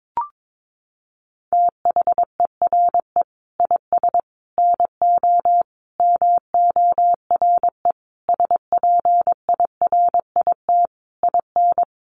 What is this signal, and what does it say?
Morse code, heard as a keyed tone, about 22 wpm, 700 Hz. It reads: THERE IS NO MORE SPIRIT IN
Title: Sentences from Top 1400 Words 22wpm